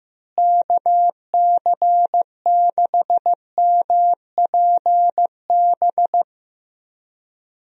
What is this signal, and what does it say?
Morse code, heard as a keyed tone, about 15 wpm, 700 Hz. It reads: KC6MPB
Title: USA Call Signs - 6 Characters - Repeated 3x - 15wpm